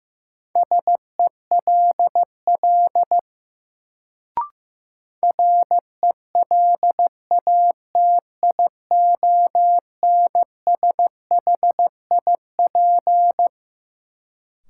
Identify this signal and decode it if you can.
Morse code, heard as a keyed tone, about 15 wpm, 700 Hz. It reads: SELL RELATIONSHIP
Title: Sentences from Top 1100 Words - New Word Drill Alt - 15wpm